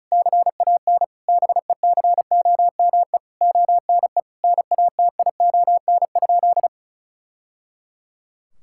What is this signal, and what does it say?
Morse code, heard as a keyed tone, about 34 wpm, 700 Hz. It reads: CAN BECOME ONE NATION?
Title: Sentences from Top 1000 Words - Review Part 6 - 35wpm